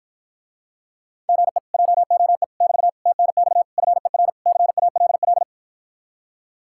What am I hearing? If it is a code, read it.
Morse code, heard as a keyed tone, about 52 wpm, 700 Hz. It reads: GE JOE <BT> TNX FER CALL